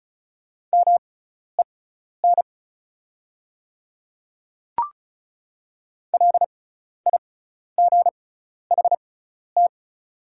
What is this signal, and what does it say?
Morse code, heard as a keyed tone, about 36 wpm, spaced as for 12 wpm, 690 Hz. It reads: MEN LIGHT
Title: 6x Character Spacing - Top 300 Words - 35wpm